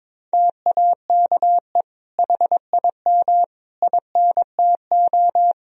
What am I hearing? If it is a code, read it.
Morse code, heard as a keyed tone, about 22 wpm, 700 Hz. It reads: TAKE HIM INTO
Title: Sentences from Top 1000 Words - Review Part 4 - 22wpm